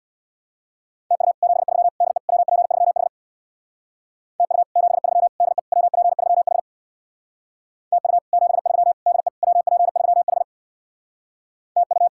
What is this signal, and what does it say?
Morse code, heard as a keyed tone, about 64 wpm, 700 Hz. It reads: TU 73 DE WP3H TU 73 DE WP3H TU 73 DE WP3H TU